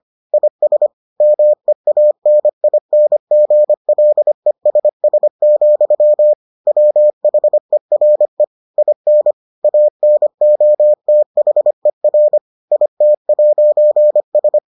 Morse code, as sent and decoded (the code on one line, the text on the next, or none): .. ... / -- . .- -. .. -. --. .-.. ... ... --..-- / .-- .... .-. / .. -. / .- -. --- - .... .-. / .. - .----. ...
IS MEANINGLESS, WHERE IN ANOTHER IT'S